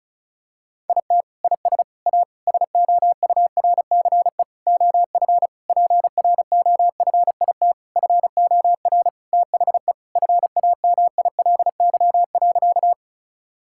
Morse code, heard as a keyed tone, about 36 wpm, 700 Hz. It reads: IT IS A SOURCE OF PROFIT FOR THE FAMILY.
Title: Sentences from Top 1000 Words - Review Part 5 - 35wpm